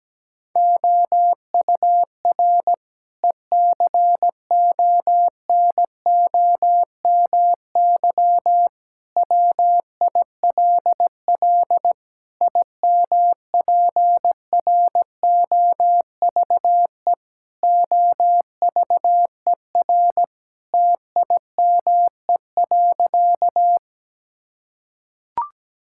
Morse code, heard as a keyed tone, about 17 wpm, 700 Hz. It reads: OUR ECONOMY WILL IMPROVE OVER TIME.
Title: Sentences from Top 1000 Words - Review Part 5 - 17wpm